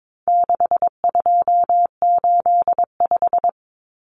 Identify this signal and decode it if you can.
Morse code, heard as a keyed tone, about 22 wpm, 700 Hz. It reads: 6285
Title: Common Cut Numbers - 4-Digit 22wpm